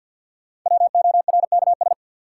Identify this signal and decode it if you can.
Morse code, heard as a keyed tone, about 51 wpm, 700 Hz. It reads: WORKS
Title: Sentences from Top 1000 Words - Review Part 5 - New Word Drill - 50wpm